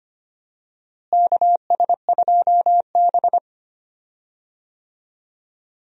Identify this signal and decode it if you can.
Morse code, heard as a keyed tone, about 25 wpm, 700 Hz. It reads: KS2B
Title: USA Call Signs - 4 Characters - Repeated 3x - 25wpm